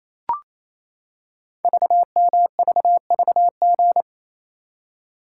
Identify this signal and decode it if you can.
Morse code, heard as a keyed tone, about 28 wpm, 700 Hz. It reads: VMVVG
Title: Code Groups - 5 Letters - 28wpm